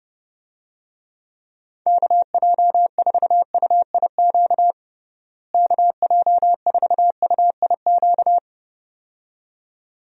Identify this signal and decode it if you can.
Morse code, heard as a keyed tone, about 30 wpm, 700 Hz. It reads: KJ4UIQ KJ4UIQ
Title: USA Call Signs - 6 Characters - Repeated 2x - 30wpm